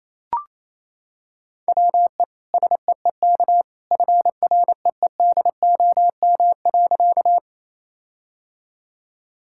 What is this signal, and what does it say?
Morse code, heard as a keyed tone, about 28 wpm, 700 Hz. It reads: WE SEEK FREEDOM.
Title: Sentences from Top 1600 Words 28wpm